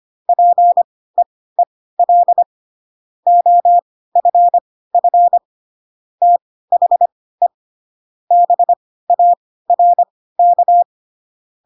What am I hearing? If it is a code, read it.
Morse code, heard as a keyed tone, about 25 wpm, spaced as for 16 wpm, 700 Hz. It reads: PEEL OFF THE BARK
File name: Pinocchio / Pinocchio ch1 10 25 ik1hge